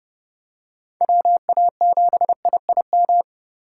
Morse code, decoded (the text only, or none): WA7IIM